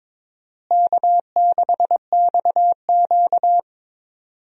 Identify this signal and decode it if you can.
Morse code, heard as a keyed tone, about 22 wpm, 700 Hz. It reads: K6XQ
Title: USA Call Signs - 4 Characters - Repeated 3x - 22wpm